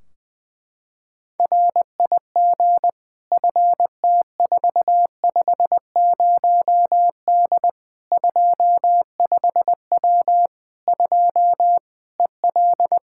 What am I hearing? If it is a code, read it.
Morse code, heard as a keyed tone, about 21 wpm, 700 Hz. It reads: RIG FT450D 25W 2 EL
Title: Practice QSO - Exchange 2 - Info - 20wpm